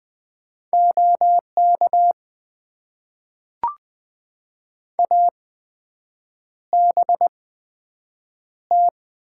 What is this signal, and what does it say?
Morse code, heard as a keyed tone, about 20 wpm, spaced as for 7 wpm, 700 Hz. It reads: OK ABT
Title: Top 90 QSO Elements - 8x Character Spacing - 20wpm